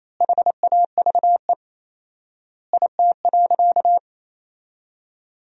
Morse code, heard as a keyed tone, about 28 wpm, 700 Hz. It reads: HAVE IT.
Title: 4x Word-Spacing - Sentences from Top 300 Words - 28wpm